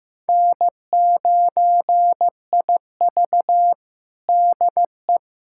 Morse code, heard as a keyed tone, about 16 wpm, 700 Hz. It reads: N9IV DE
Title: USA Callsign DE Callsign - 4 Characters - 15wpm